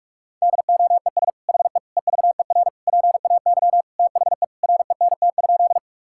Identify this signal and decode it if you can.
Morse code, heard as a keyed tone, about 46 wpm, 690 Hz. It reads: DOES HE EVER PAY THE RENT?
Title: Sentences from Top 1000 Words - Review Part 5 - 45wpm